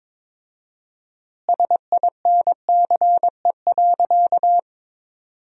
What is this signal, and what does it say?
Morse code, heard as a keyed tone, about 22 wpm, 700 Hz. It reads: SINCE.
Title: 4x Word-Spacing - Sentences from Top 300 Words - 22wpm